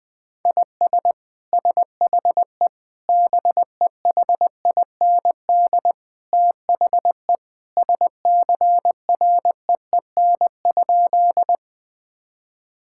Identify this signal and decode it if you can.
Morse code, heard as a keyed tone, about 21 wpm, 700 Hz. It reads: IS SHE BEHIND THE SCREEN?
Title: Sentences from Top 1000 Words - Review Part 6 - 20wpm